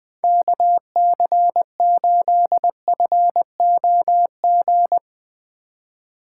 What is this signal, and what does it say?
Morse code, heard as a keyed tone, about 20 wpm, 700 Hz. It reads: KC8FOG
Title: USA Call Signs - 6 Characters - 3x Character Spacing - 20wpm